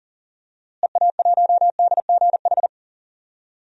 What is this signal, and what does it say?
Morse code, heard as a keyed tone, about 40 wpm, 700 Hz. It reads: EA1DGH
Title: DXCC-100 Call Signs - 6 Characters - Repeated 3x - 40wpm